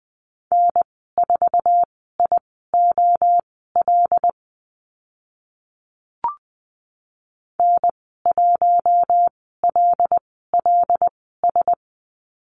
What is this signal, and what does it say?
Morse code, heard as a keyed tone, about 20 wpm, spaced as for 14 wpm, 700 Hz. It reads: N4IOL N1LLS
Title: 076 Lesson 13g - Letter L - Call Signs 20wpm